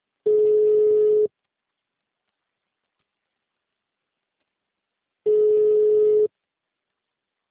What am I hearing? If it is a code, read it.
Ringback tone (Europe, 425 Hz).